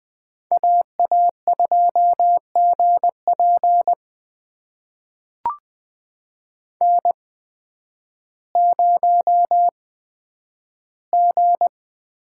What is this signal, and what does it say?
Morse code, heard as a keyed tone, about 20 wpm, spaced as for 11 wpm, 700 Hz. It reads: AA2GP N0G
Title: USA Call Signs - 8x Character Spacing - 20wpm